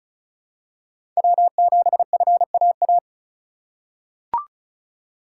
Morse code, heard as a keyed tone, about 34 wpm, 700 Hz. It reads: W7FAA
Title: USA Call Signs - 5 Characters - 2x Character Spacing - 35wpm